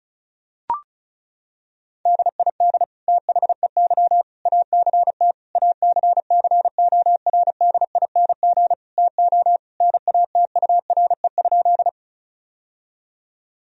Morse code, heard as a keyed tone, about 34 wpm, 700 Hz. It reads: DID THEY ACT ACCORDING TO NATURE?
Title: Sentences from Top 1000 Words - Review Part 4 - 35wpm